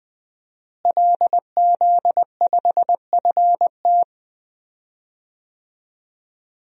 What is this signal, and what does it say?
Morse code, heard as a keyed tone, about 20 wpm, 700 Hz. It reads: LZ5FT
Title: DXCC-100 Call Signs - 5 Characters - Repeated 2x - 20wpm